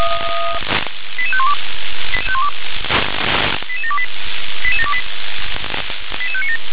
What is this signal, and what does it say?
Five-tone selective call (ZVEI), heard as quick runs of five tones, tones of about 70 ms. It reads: 88512 88728 88728 88597